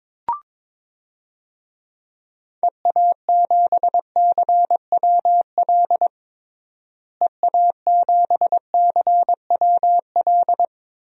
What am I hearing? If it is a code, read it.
Morse code, heard as a keyed tone, about 22 wpm, 700 Hz. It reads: EA7CWL EA7CWL
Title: DXCC-100 Call Signs - 6 Characters - Repeated 3x - 22wpm